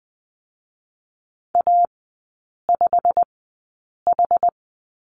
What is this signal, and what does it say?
Morse code, heard as a keyed tone, about 20 wpm, spaced as for 8 wpm, 700 Hz. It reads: A5H
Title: 087 Lesson 15e - Number 5 - Three Characters 20wpm